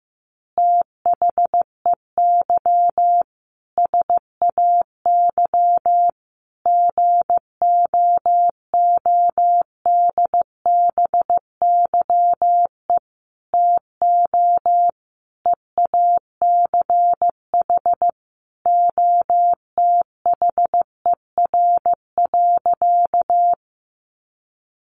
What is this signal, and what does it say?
Morse code, heard as a keyed tone, about 15 wpm, 700 Hz. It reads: THEY SAY GOODBYE TO EACH OTHER.
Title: Additional Easy Phrases 15wpm